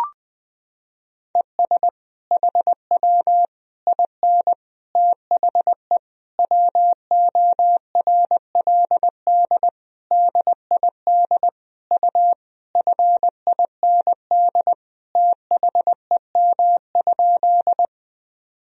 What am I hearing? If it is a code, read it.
Morse code, heard as a keyed tone, about 20 wpm, 700 Hz. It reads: ES HW IN THE WORLD DID U FIND THEM?
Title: Sentences from Top 200 Words and Abbreviations 20wpm